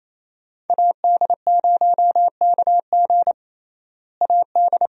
Morse code, heard as a keyed tone, about 28 wpm, 700 Hz. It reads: AD0KG AD
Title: USA Call Signs - 5 Characters - Repeated 3x - 28wpm